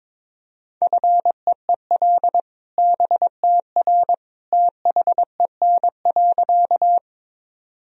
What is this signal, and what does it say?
Morse code, heard as a keyed tone, about 22 wpm, 700 Hz. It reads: FEEL BTR THEN.
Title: Sentences from Top 400 Words and Abbreviations 22wpm